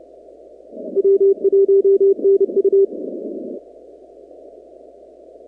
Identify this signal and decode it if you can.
Morse code, heard as a keyed tone, about 28 wpm, 390 Hz. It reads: W1NU
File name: contests / 2019-cw-wpx-kt5j-k5pi- -2330 0-46